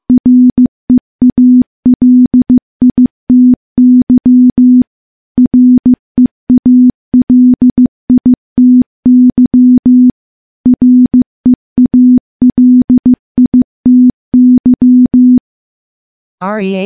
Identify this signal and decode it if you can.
Morse code, heard as a keyed tone, about 15 wpm, 250 Hz. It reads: REALITY REALITY REALITY
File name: morse-training-15WPM